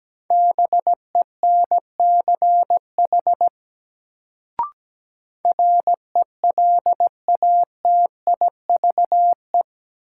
Morse code, heard as a keyed tone, about 17 wpm, 700 Hz. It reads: BENCH RELATIVE
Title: Sentences from Top 1400 Words - New Word Drill Alt - 17wpm